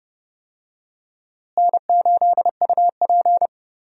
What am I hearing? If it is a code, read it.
Morse code, heard as a keyed tone, about 30 wpm, 700 Hz. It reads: N8UP
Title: USA Call Signs - 4 Characters - Repeated 3x - 30wpm